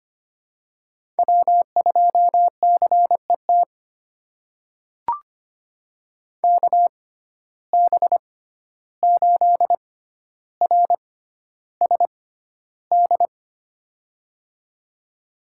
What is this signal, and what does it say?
Morse code, heard as a keyed tone, about 25 wpm, spaced as for 12 wpm, 700 Hz. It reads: W2CET KB8RSD
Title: USA Call Signs - 6x Character Spacing - 25wpm